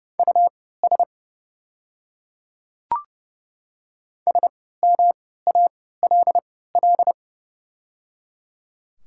Morse code, heard as a keyed tone, about 30 wpm, spaced as for 17 wpm, 700 Hz. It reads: US SMALL
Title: 3x Character Spacing - Top 200 Words - 30wpm